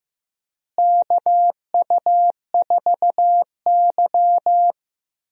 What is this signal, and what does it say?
Morse code, heard as a keyed tone, about 15 wpm, 700 Hz. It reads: KU4Y
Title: USA Call Signs - 4 Characters - Repeated 2x - 15wpm